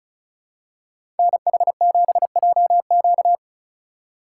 Morse code, heard as a keyed tone, about 36 wpm, 700 Hz. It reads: NH7JQ
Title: USA Call Signs - 5 Characters - 4x Character Spacing - 35wpm